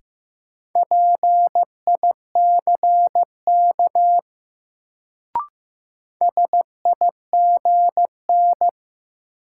Morse code, heard as a keyed tone, about 15 wpm, 700 Hz. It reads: PICK SIGN
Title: Sentences from Top 1000 Words - New Word Drill - 15wpm